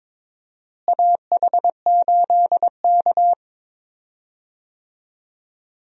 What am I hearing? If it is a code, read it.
Morse code, heard as a keyed tone, about 21 wpm, 700 Hz. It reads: AH8K
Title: USA Call Signs - 4 Characters - Repeated 3x - 22wpm